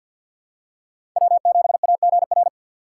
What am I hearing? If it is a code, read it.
Morse code, heard as a keyed tone, about 52 wpm, 700 Hz. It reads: W7AGR
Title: USA Call Signs - 5 Characters - 4x Character Spacing - 50wpm